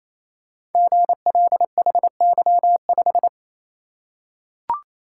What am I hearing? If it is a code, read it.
Morse code, heard as a keyed tone, about 27 wpm, 700 Hz. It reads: GLHY5
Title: Code Groups - 5 Letter:Numbers - 28wpm